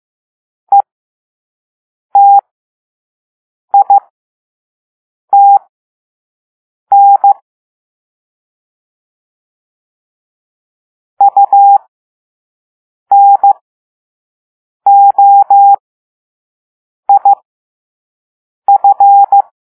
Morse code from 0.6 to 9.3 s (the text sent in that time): ETITN